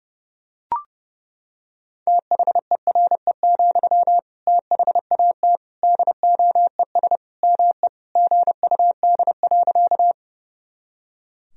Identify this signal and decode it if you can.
Morse code, heard as a keyed tone, about 30 wpm, 700 Hz. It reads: THERE, THAT DOES ME GUD.
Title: Sentences from Top 200 Words and Abbreviations 30wpm